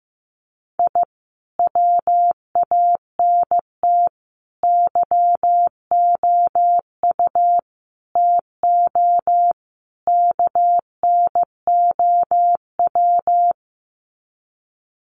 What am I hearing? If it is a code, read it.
Morse code, heard as a keyed tone, about 15 wpm, 700 Hz. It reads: I WANT YOU TO KNOW
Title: Sets of 5 Words Encore 15wpm